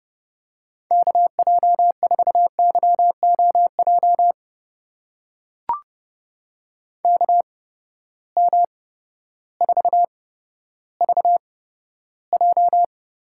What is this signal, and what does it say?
Morse code, heard as a keyed tone, about 30 wpm, spaced as for 14 wpm, 700 Hz. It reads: KJ4YOJ KM4VJ
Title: USA Call Signs - 6 Characters - 8x Character Spacing - 30wpm